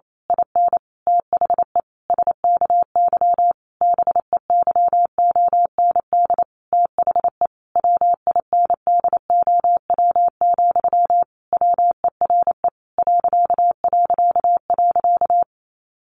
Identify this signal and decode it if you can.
Morse code, heard as a keyed tone, about 28 wpm, 700 Hz. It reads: IN THE SKY BEYOND THE WINDOW, WERE ...
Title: A Sound Of Thunder 28wpm